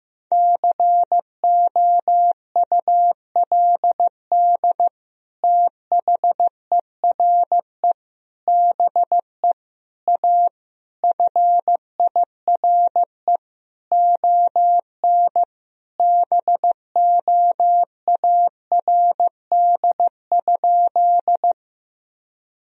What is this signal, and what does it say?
Morse code, heard as a keyed tone, about 15 wpm, 700 Hz. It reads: COULD THERE BE A FIRE ON BOARD?
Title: Sentences from Top 1000 Words - Review Part 4 - 15wpm